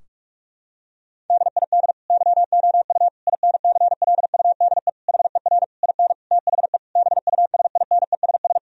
Morse code, heard as a keyed tone, about 41 wpm, 700 Hz. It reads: DID YOU INCLUDE HER IN THE BUSINESS
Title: Sentences from Top 1000 Words - Review Part 3 - 45wpm